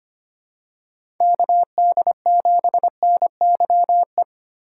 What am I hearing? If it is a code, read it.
Morse code, heard as a keyed tone, about 25 wpm, 700 Hz. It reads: KD7NYE
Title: USA Call Signs - 6 Characters - 3x Character Spacing - 25wpm